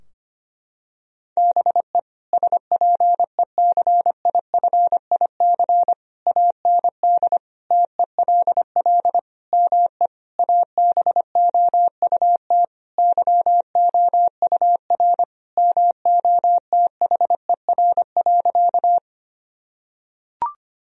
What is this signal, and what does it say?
Morse code, heard as a keyed tone, about 25 wpm, 700 Hz. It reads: BE SPECIFIC AND TELL ME ABOUT YOUR MOTHER.
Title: Sentences from Top 1000 Words - Review Part 3 - 25wpm